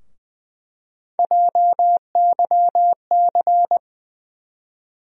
Morse code, heard as a keyed tone, about 20 wpm, 700 Hz. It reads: JYC
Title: USA Call Sign Postfixes - 3 Letters 20wpm